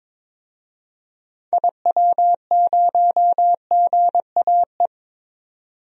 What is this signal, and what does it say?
Morse code, heard as a keyed tone, about 21 wpm, 700 Hz. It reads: IW0GAE
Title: DXCC-100 Call Signs - 6 Characters - 22wpm